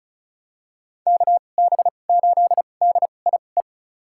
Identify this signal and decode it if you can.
Morse code, heard as a keyed tone, about 36 wpm, spaced as for 25 wpm, 700 Hz. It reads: KB8DIE
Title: USA Call Signs - 6 Characters - 2x Character Spacing - 35wpm